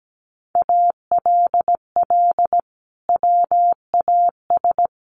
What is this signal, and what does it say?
Morse code, heard as a keyed tone, about 17 wpm, 700 Hz. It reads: ALL WAS
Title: Sentences from Top 500 Words 17wpm